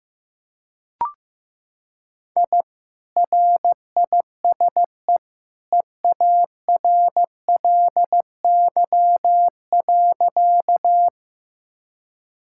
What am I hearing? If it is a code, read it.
Morse code, heard as a keyed tone, about 15 wpm, 700 Hz. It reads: I RISE EARLY.
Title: Sentences from Top 1000 Words - Review Part 6 - 15wpm